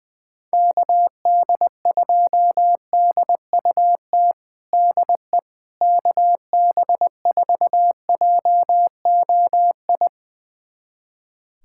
Morse code, heard as a keyed tone, about 20 wpm, 700 Hz. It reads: KD2DUT DE KB4JOI
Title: USA Callsign DE Callsign - 6 Characters - 20wpm